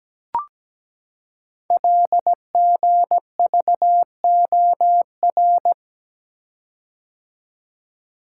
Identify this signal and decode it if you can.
Morse code, heard as a keyed tone, about 17 wpm, 700 Hz. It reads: LGVOR
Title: Code Groups - 5 Letters - 17wpm